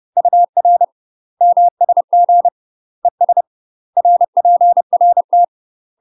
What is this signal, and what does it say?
Morse code, heard as a keyed tone, about 30 wpm, 700 Hz. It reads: UR MSG ES RPRT
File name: UK MASTER QSOs TRAINING / 30WPM / UK MASTER QSOs 0036